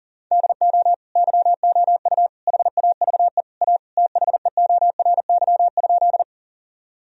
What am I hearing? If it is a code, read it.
Morse code, heard as a keyed tone, about 40 wpm, 700 Hz. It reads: DO YOU HAVE A THEORY?
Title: Sentences from Top 1000 Words - Review Part 3 - 40wpm